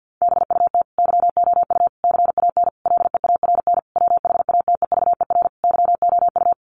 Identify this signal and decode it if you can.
Morse code, heard as a keyed tone, about 50 wpm, 700 Hz. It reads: BUT YOU CAN LEARN WHATEVER YOU
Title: Sentences from Top 1500 Words 50wpm